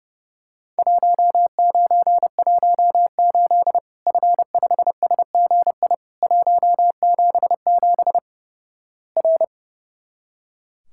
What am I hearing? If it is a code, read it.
Morse code, heard as a keyed tone, about 32 wpm, 700 Hz. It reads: 1918 F5SGI 177 R
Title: Worked All Europe DX Contest - QTC Exchange - 30wpm